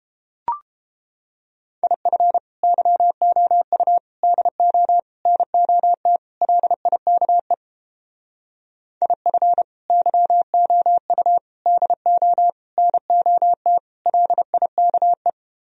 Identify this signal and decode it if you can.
Morse code, heard as a keyed tone, about 29 wpm, 700 Hz. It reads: IF YOU DO NOT LIKE IF YOU DO NOT LIKE